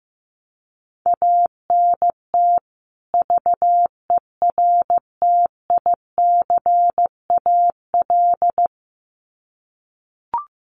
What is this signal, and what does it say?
Morse code, heard as a keyed tone, about 15 wpm, 700 Hz. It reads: ANT VERTICAL
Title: Common QSO Phrases 15wpm